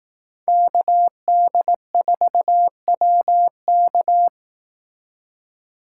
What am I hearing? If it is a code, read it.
Morse code, heard as a keyed tone, about 18 wpm, 700 Hz. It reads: KD4WK
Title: Speed Racing - USA 2x2 Call Signs 15wpm